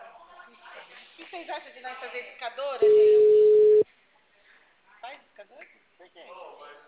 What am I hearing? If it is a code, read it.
Ringback tone (Europe, 425 Hz), with speech.
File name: out- -2002- - - .48706